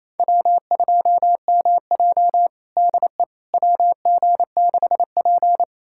Morse code, heard as a keyed tone, about 28 wpm, 700 Hz. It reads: W2MJ DE WG6P
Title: USA Callsign DE Callsign - 4 Characters - 28wpm